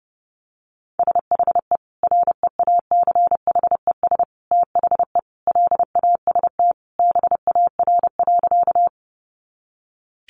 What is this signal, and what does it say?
Morse code, heard as a keyed tone, about 30 wpm, 700 Hz. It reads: SHE REACHES THE LAST BAR.
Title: Easy Phrases 30wpm